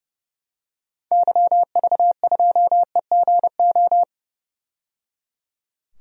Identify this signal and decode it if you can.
Morse code, heard as a keyed tone, about 30 wpm, 700 Hz. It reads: YV2EGO
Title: DXCC-100 Call Signs - 6 Characters - 30wpm